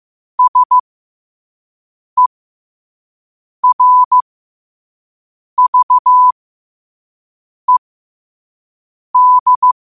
Morse code, heard as a keyed tone, about 15 wpm, spaced as for 5 wpm, 1000 Hz. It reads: SERVED
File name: 5wpm